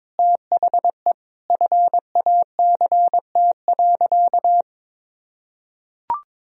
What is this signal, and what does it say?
Morse code, heard as a keyed tone, about 21 wpm, 700 Hz. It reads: THE FACT.
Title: Sentences from Top 1000 Words - Review Part 3 - 22wpm